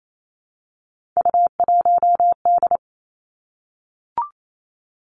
Morse code, heard as a keyed tone, about 28 wpm, 700 Hz. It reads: U1D
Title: Three Challenging Letter-Number Combinations 28wpm